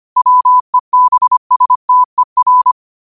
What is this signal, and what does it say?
Morse code, heard as a keyed tone, about 25 wpm, 1000 Hz. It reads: WEBSTER
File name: qotdfiles / 25wpm